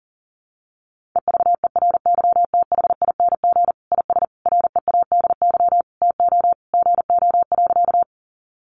Morse code, heard as a keyed tone, about 40 wpm, 700 Hz. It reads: EVERYTHING IS READY TO GO.
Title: Sentences from Top 700 Words 40wpm